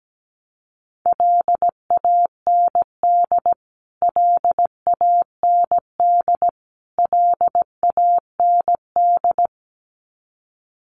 Morse code, heard as a keyed tone, about 17 wpm, 700 Hz. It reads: LAND LAND LAND
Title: Top 500 Words Repeated 17wpm